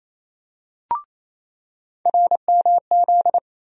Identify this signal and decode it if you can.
Morse code, heard as a keyed tone, about 28 wpm, 700 Hz. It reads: RMZ
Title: USA Call Sign Postfixes - 3 Letters 28wpm